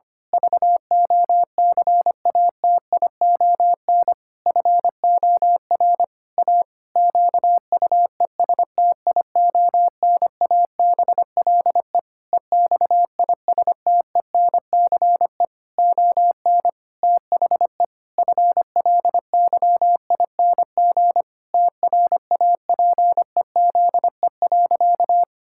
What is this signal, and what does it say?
Morse code, heard as a keyed tone, about 25 wpm, 700 Hz. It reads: VOCATION FOR A QUESTIONABLE EXISTENCE ON THE FLYING TRAPEZE.
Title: Pangram Challenge v2 - 25wpm